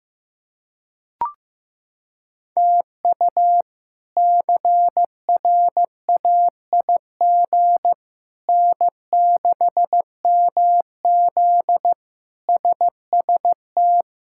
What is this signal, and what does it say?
Morse code, heard as a keyed tone, about 16 wpm, 700 Hz. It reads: TU CRAIG N6MZ SST
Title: K1USN SST QSO Practice - 15wpm